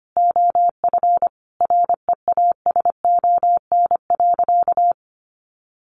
Morse code, heard as a keyed tone, about 25 wpm, 700 Hz. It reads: OF REASON.
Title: Sentences from Top 1300 Words 25wpm